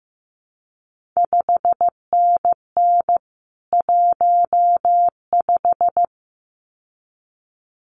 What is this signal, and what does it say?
Morse code, heard as a keyed tone, about 15 wpm, 700 Hz. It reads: 5NN 15
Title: CQ WW DX Contest Exchange Encore - 15wpm